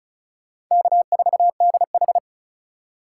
Morse code, heard as a keyed tone, about 36 wpm, 700 Hz. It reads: K4DH
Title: USA Call Signs - 4 Characters - 2x Character Spacing - 35wpm